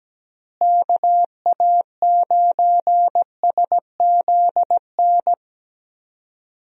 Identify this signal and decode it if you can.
Morse code, heard as a keyed tone, about 17 wpm, 700 Hz. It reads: KA9SZN
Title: USA Call Signs - 6 Characters - Repeated 2x - 17wpm